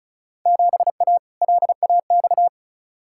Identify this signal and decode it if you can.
Morse code, heard as a keyed tone, about 36 wpm, 700 Hz. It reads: 7A LAX
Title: Field Day Exchanges Encore - 35wpm